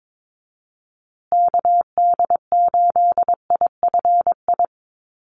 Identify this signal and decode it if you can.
Morse code, heard as a keyed tone, about 22 wpm, 700 Hz. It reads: KD8IFI
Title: USA Call Signs Repeated 2x 22wpm